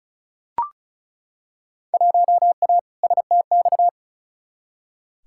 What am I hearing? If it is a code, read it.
Morse code, heard as a keyed tone, about 36 wpm, 700 Hz. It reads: 1A STX
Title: Field Day Exchanges Encore - 35wpm